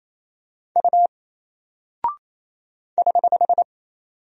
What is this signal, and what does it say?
Morse code, heard as a keyed tone, about 27 wpm, 700 Hz. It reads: U <HH>
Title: Single Character (Letter:Number:Punctuation:Prosign) Alt 28wpm